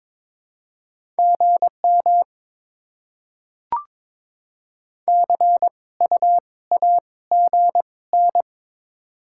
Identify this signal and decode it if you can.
Morse code, heard as a keyed tone, about 22 wpm, spaced as for 17 wpm, 690 Hz. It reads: GM CU AGN
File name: Top 20 QSO Elements - 2x Character Spacing - 22wpm